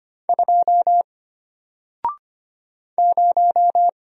Morse code, heard as a keyed tone, about 25 wpm, 700 Hz. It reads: 2 0
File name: Single-Letter-Number-25wpm